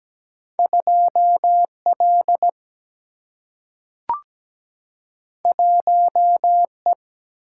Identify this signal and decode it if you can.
Morse code, heard as a keyed tone, about 17 wpm, 700 Hz. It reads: 2 L 1 E
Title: Single-Letter Single-Number 17wpm